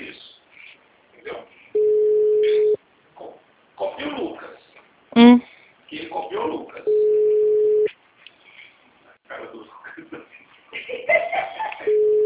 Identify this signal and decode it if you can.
Ringback tone (Europe, 425 Hz), with speech.